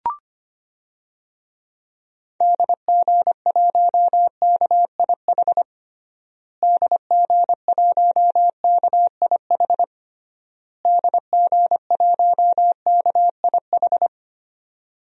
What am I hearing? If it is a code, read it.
Morse code, heard as a keyed tone, about 25 wpm, 700 Hz. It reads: DG1KIH DG1KIH DG1KIH
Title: DXCC-100 Call Signs - 6 Characters - Repeated 3x - 25wpm